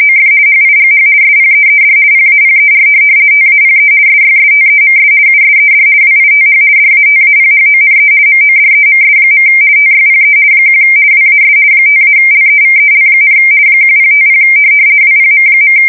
RTTY, about 50 baud. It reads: OF HIGH SPEED PAPER TAPE SYSTEMS USED WITH DATAPHONE MODEMS. TYPE 1 WAS 5-LEVEL, 1050 WPM.
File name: fsk170-50B / teletype-corp